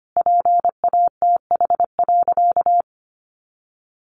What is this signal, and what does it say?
Morse code, heard as a keyed tone, about 25 wpm, 700 Hz. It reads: PATH.
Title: A Sound Of Thunder 25wpm